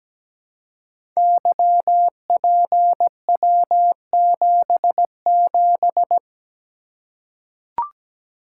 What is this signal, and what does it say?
Morse code, heard as a keyed tone, about 17 wpm, 700 Hz. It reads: YPW77
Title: Code Groups - 5 Letter:Numbers - 17wpm